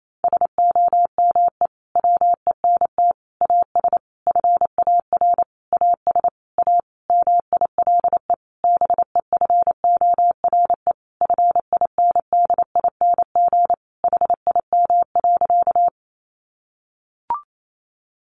Morse code, heard as a keyed tone, about 28 wpm, 700 Hz. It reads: SOME WENT AS FAR AS A MILE BEFORE FINDING HIM.
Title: Sentences from Top 1100 Words 28wpm